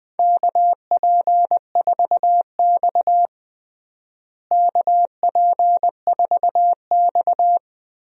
Morse code, heard as a keyed tone, about 20 wpm, 700 Hz. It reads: KP4X KP4X
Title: DXCC-100 Call Signs - 4 Characters - Repeated 3x - 20wpm